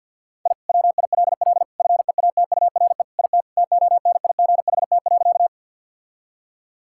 Morse code, heard as a keyed tone, about 50 wpm, 700 Hz. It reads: I WILL FEATURE IT TONIGHT.
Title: Sentences from Top 1000 Words - Review Part 5 - 50wpm